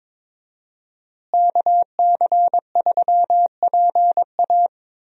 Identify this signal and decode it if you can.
Morse code, heard as a keyed tone, about 22 wpm, 700 Hz. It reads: KC3PA
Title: USA Call Signs - 5 Characters - Repeated 3x - 22wpm